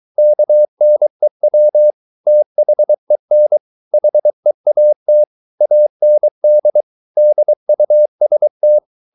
Morse code, heard as a keyed tone, about 23 wpm, 600 Hz. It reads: KNEW THEN HEAT AND DUST